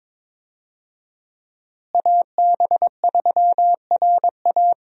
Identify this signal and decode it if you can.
Morse code, heard as a keyed tone, about 22 wpm, 700 Hz. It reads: AB3RA
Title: USA Call Signs - 5 Characters - Repeated 3x - 22wpm